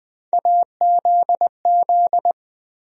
Morse code, heard as a keyed tone, about 20 wpm, 700 Hz. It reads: AZZ
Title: USA Call Sign Postfixes - 3 Letters 20wpm